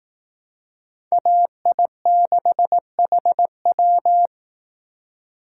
Morse code, heard as a keyed tone, about 18 wpm, 700 Hz. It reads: AI6HW
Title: Speed Racing - USA 2x2 Call Signs 15wpm